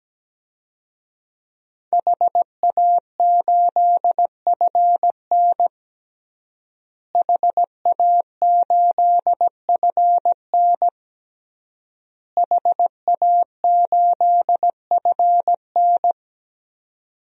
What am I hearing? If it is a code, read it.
Morse code, heard as a keyed tone, about 17 wpm, 700 Hz. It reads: HA8FN HA8FN HA8FN
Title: DXCC-100 Call Signs - 5 Characters - Repeated 3x - 17wpm